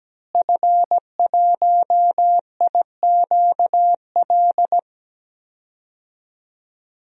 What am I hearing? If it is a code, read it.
Morse code, heard as a keyed tone, about 17 wpm, 700 Hz. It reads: F1IQL
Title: DXCC-100 Call Signs - 5 Characters - Repeated 2x - 17wpm